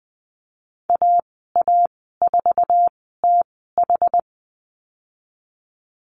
Morse code, heard as a keyed tone, about 20 wpm, spaced as for 14 wpm, 700 Hz. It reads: AA4TH
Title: 062 Lesson 11g - Letter h - Call Signs 20wpm